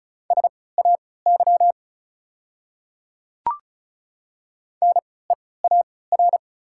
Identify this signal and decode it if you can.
Morse code, heard as a keyed tone, about 34 wpm, spaced as for 20 wpm, 700 Hz. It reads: SAY NEAR
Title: 3x Character Spacing - Top 300 Words - 35wpm